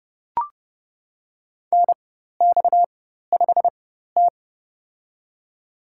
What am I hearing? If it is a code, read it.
Morse code, heard as a keyed tone, about 30 wpm, spaced as for 14 wpm, 700 Hz. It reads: NX5T
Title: USA Call Signs - 4 Characters - 4x Character Spacing - 30wpm